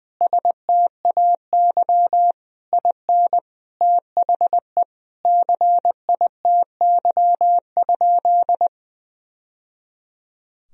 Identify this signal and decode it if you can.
Morse code, heard as a keyed tone, about 20 wpm, 700 Hz. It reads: STAY IN THE CITY?
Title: Sentences from Top 1000 Words - Review Part 1 - 20wpm